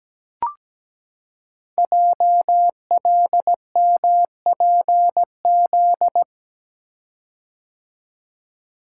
Morse code, heard as a keyed tone, about 17 wpm, 700 Hz. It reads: JLMPZ
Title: Code Groups - 5 Letters - 17wpm